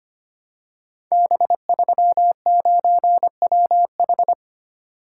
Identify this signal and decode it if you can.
Morse code, heard as a keyed tone, about 25 wpm, 700 Hz. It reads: B39WH
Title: Code Groups - 5 Letter:Numbers - 25wpm